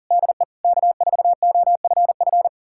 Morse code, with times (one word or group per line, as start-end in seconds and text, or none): 0.1-0.4 DE
0.6-2.5 K4OFF